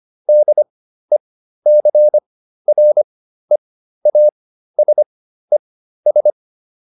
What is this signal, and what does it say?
Morse code, heard as a keyed tone, about 25 wpm, spaced as for 13 wpm, 600 Hz. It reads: DECREASES